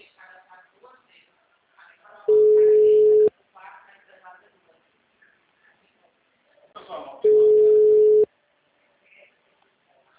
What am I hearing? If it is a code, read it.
Ringback tone (Europe, 425 Hz), with speech.
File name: out- -2010- - - .26689